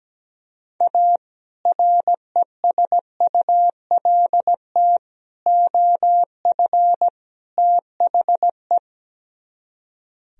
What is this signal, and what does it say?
Morse code, heard as a keyed tone, about 17 wpm, 700 Hz. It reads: A RESULT OF THE
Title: Speed-Racing - Sets of 4 Words 17wpm